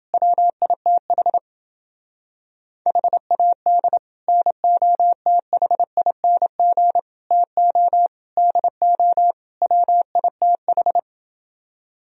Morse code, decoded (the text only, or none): WITH HAD NOTHING TO DO WITH